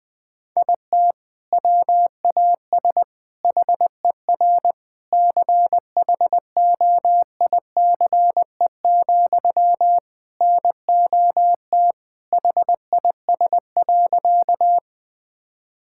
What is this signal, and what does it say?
Morse code, heard as a keyed tone, about 20 wpm, 700 Hz. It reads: IT WAS HER CHOICE, NOT HIS.
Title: Sentences from Top 1000 Words - Review Part 6 - 20wpm